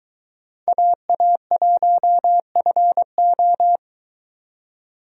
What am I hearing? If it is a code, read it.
Morse code, heard as a keyed tone, about 24 wpm, 700 Hz. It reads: AA1FO